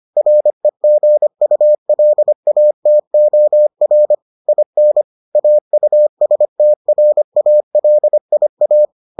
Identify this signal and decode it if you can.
Morse code, heard as a keyed tone, about 25 wpm, 600 Hz. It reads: REGULATOR IN AUSTRALIA